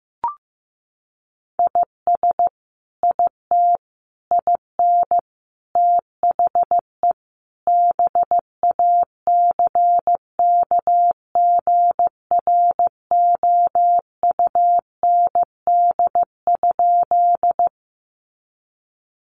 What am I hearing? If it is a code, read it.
Morse code, heard as a keyed tone, about 15 wpm, 700 Hz. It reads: IS IT IN THE BACKGROUND?
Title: Sentences from Top 1300 Words 15wpm